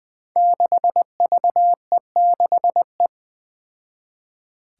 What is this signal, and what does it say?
Morse code, heard as a keyed tone, about 20 wpm, 700 Hz. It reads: 6VE6E
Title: Five Challenging Letter-Number Combinations 20wpm